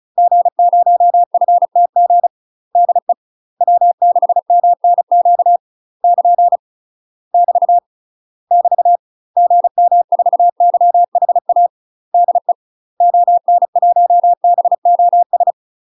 Morse code, heard as a keyed tone, about 36 wpm, 700 Hz. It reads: G0FTG DE W6MNQ <KN> <BT> <BT> GM4YHA DE ON1BOS
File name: UK MASTER QSOs TRAINING / 35WPM / UK MASTER QSOs 0035